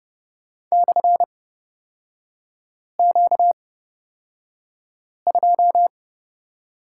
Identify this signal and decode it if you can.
Morse code, heard as a keyed tone, about 30 wpm, spaced as for 5 wpm, 700 Hz. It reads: /Q2
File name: Lesson 38e - Letter q - Three Characters 30wpm